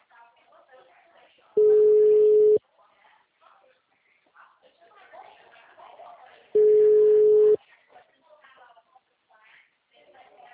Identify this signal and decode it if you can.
Ringback tone (Europe, 425 Hz), with speech.
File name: out- -2019- - - .43837